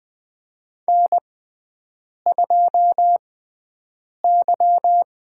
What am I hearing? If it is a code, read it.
Morse code, heard as a keyed tone, about 21 wpm, spaced as for 7 wpm, 700 Hz. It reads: N2Y
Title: USA Call Signs - 6x Character Spacing - 20wpm